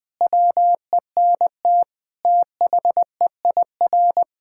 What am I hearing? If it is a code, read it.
Morse code, heard as a keyed tone, about 20 wpm, 700 Hz. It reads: WENT THEIR
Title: Sentences from Top 1000 Words - Review Part 5 - 20wpm